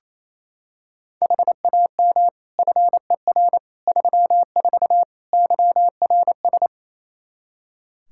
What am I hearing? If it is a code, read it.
Morse code, heard as a keyed tone, about 28 wpm, 700 Hz. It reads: HAM FER 34 YRS
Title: Practice QSO - Exchange 3 - More Info - 28wpm